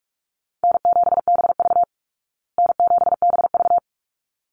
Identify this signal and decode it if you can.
Morse code, heard as a keyed tone, about 44 wpm, 700 Hz. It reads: N7BV N7BV
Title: Easy Call Signs Repeated 2x - 45wpm